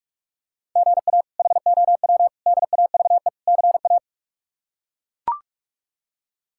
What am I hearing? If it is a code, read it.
Morse code, heard as a keyed tone, about 41 wpm, 700 Hz. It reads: GA HOW DAVE CA
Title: K1USN SST QSO Practice - 45wpm